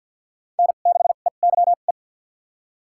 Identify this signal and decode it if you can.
Morse code, heard as a keyed tone, about 45 wpm, spaced as for 34 wpm, 700 Hz. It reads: N6EYE
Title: USA Call Signs - 5 Characters - 2x Character Spacing - 50wpm